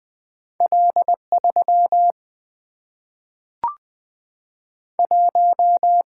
Morse code, heard as a keyed tone, about 20 wpm, 700 Hz. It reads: L3 1
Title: Single-Letter Single-Number 20wpm